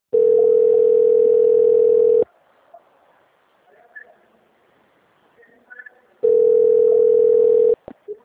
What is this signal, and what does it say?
Ringback tone (North America), 440 + 480 Hz.
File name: out- - - - - .12205